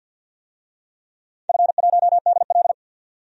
Morse code, heard as a keyed tone, about 52 wpm, 700 Hz. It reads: F1DL
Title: DXCC-100 Call Signs - 4 Characters - 50wpm